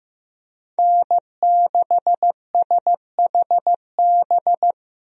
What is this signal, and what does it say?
Morse code, heard as a keyed tone, about 15 wpm, 700 Hz. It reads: N6SHB
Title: USA Call Signs - 5 Characters - Repeated 2x - 15wpm